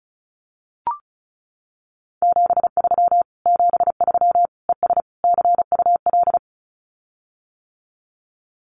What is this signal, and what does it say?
Morse code, heard as a keyed tone, about 34 wpm, 700 Hz. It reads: 73 73 ES CUL
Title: Common QSO Phrases Encore 35wpm